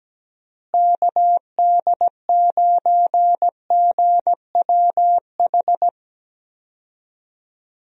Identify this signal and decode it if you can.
Morse code, heard as a keyed tone, about 17 wpm, 700 Hz. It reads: KD9GWH
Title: USA Call Signs - 6 Characters - Repeated 2x - 17wpm